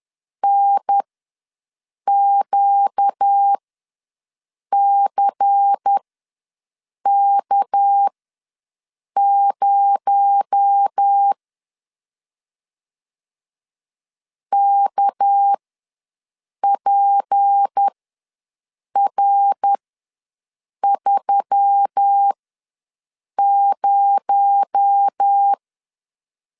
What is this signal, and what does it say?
Morse code, heard as a keyed tone, about 10 wpm, spaced as for 6 wpm, 800 Hz. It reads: NQCK0 KPR30
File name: LEZIONI / 6 PKX0 / 6 PKX0 ws 10wpm 4220